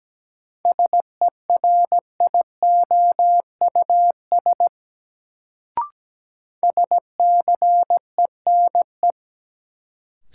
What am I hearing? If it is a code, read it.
Morse code, heard as a keyed tone, about 17 wpm, 700 Hz. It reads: SERIOUS SCENE
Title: Sentences from Top 1100 Words - New Word Drill Alt - 17wpm